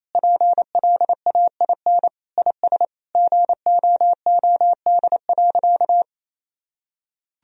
Morse code, heard as a keyed tone, about 28 wpm, 700 Hz. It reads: PLAIN IS GOOD.
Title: Sentences from Top 1000 Words - Review Part 4 - 28wpm